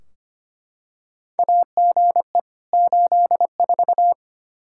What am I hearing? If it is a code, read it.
Morse code, heard as a keyed tone, about 25 wpm, 700 Hz. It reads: AGE 84
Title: Practice QSO - Exchange 3 - More Info - 25wpm